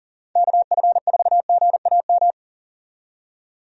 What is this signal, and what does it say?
Morse code, heard as a keyed tone, about 40 wpm, 690 Hz. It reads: KF4GAM
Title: USA Call Signs - 6 Characters - 40wpm